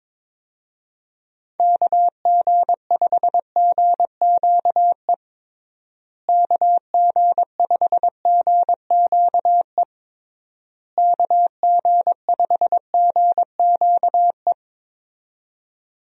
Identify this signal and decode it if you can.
Morse code, heard as a keyed tone, about 22 wpm, 700 Hz. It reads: KG5GQE KG5GQE KG5GQE
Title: USA Call Signs - 6 Characters - Repeated 3x - 22wpm